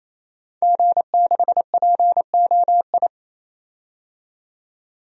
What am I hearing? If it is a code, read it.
Morse code, heard as a keyed tone, about 27 wpm, 700 Hz. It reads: G6POI
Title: DXCC-100 Call Signs - 5 Characters - 28wpm